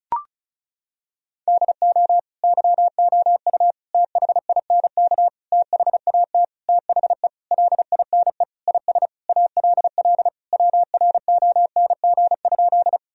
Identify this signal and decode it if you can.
Morse code, heard as a keyed tone, about 34 wpm, 700 Hz. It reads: DO YOU THINK THAT THE LINE IS ALL WRONG?
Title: Sentences from Top 1000 Words - Review Part 1 - 35wpm